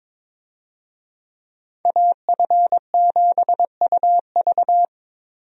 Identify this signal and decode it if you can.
Morse code, heard as a keyed tone, about 22 wpm, 700 Hz. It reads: AF7UV
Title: USA Call Signs - 5 Characters - Repeated 2x - 22wpm